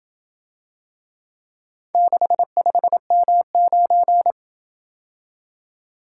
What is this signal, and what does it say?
Morse code, heard as a keyed tone, about 27 wpm, 700 Hz. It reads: 65M9